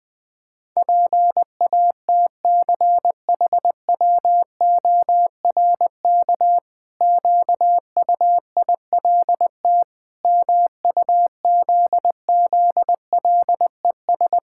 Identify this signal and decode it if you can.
Morse code, heard as a keyed tone, about 20 wpm, 700 Hz. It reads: PATCHWORK QUILT MUZZLES
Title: Pangram Challenge v2 - 20wpm